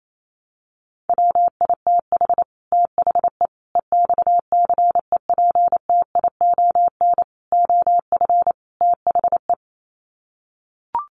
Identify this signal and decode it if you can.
Morse code, heard as a keyed tone, about 28 wpm, 700 Hz. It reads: WITH THE EXCEPTION OF THE
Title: Sets of 5 Words Encore 28wpm